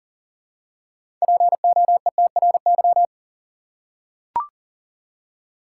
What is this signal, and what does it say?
Morse code, heard as a keyed tone, about 40 wpm, 700 Hz. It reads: POETRY
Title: Six Letter Words - 40wpm